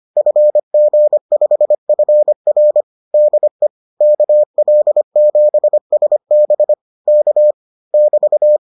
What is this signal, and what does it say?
Morse code, heard as a keyed tone, about 25 wpm, 600 Hz. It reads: FG5FR DE KL7SB K <BT>